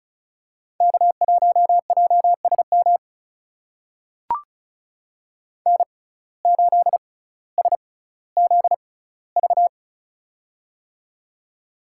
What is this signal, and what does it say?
Morse code, heard as a keyed tone, about 36 wpm, spaced as for 18 wpm, 700 Hz. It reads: K1JSM N8SZV
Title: USA Call Signs - 5 Characters - 6x Character Spacing - 35wpm